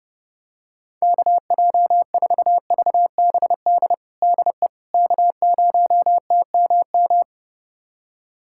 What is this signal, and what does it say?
Morse code, heard as a keyed tone, about 30 wpm, 700 Hz. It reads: KJ4VBD DE K0TMM
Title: USA Callsign DE Callsign 30wpm